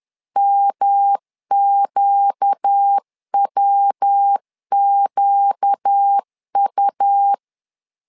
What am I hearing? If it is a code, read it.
Morse code, heard as a keyed tone, about 10 wpm, 800 Hz. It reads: MQWQU
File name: LEZIONI / 4 Q 10wpm 2210